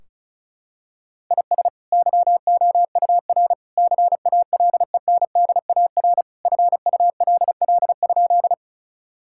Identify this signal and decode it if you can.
Morse code, heard as a keyed tone, about 36 wpm, 700 Hz. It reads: IS YOUR CALENDAR FULL?
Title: Sentences from Top 1600 Words 35wpm